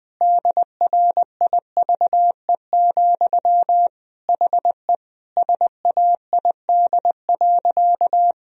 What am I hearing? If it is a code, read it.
Morse code, heard as a keyed tone, about 20 wpm, 700 Hz. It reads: DRIVE, HE SAID.
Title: Sentences from Top 1000 Words - Review Part 2 - 20wpm